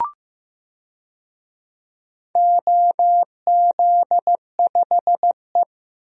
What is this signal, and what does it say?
Morse code, heard as a keyed tone, about 15 wpm, 700 Hz. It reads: OZ5E
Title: DXCC-100 Call Signs - 4 Characters - Repeated 3x - 15wpm